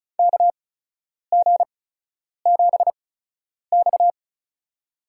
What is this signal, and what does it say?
Morse code, heard as a keyed tone, about 34 wpm, spaced as for 10 wpm, 700 Hz. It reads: KG7X
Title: USA Call Signs - 4 Characters - 8x Character Spacing - 35wpm